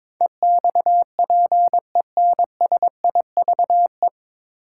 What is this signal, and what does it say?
Morse code, heard as a keyed tone, about 21 wpm, 700 Hz. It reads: EXPENSIVE
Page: Morse Code Ninja practice 22wpm